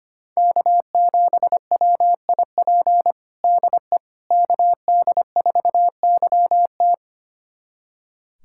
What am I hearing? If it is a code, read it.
Morse code, heard as a keyed tone, about 25 wpm, 700 Hz. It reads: K7WIP DE KD4YT
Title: USA Callsign DE Callsign - 5 Characters - 25wpm